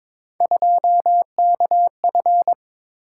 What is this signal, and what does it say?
Morse code, heard as a keyed tone, about 22 wpm, 700 Hz. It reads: 2KF
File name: practice / 22wpm / Three Letter-Number Combinations - 22wpm